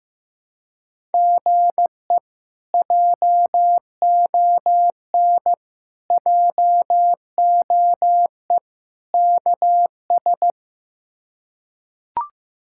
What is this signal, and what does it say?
Morse code, heard as a keyed tone, about 15 wpm, 700 Hz. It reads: GE JON JOE KS
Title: K1USN SST QSO Practice - 15wpm